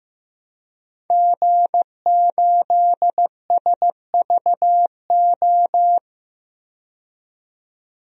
Morse code, heard as a keyed tone, about 16 wpm, 700 Hz. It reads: G8SVO
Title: DXCC-100 Call Signs - 5 Characters - 15wpm